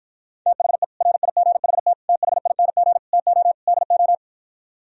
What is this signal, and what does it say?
Morse code, heard as a keyed tone, about 52 wpm, 700 Hz. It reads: THE RIGHT THING TO DO